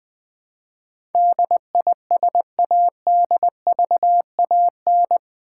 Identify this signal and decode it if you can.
Morse code, heard as a keyed tone, about 20 wpm, 700 Hz. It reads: DISADVAN
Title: Build A Word - 4 Letters At A Time - 12-Letter Words - 20wpm